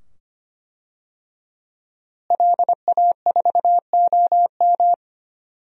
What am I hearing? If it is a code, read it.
Morse code, heard as a keyed tone, about 25 wpm, 700 Hz. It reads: LA4OM
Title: DXCC-100 Call Signs - 5 Characters - Repeated 2x - 25wpm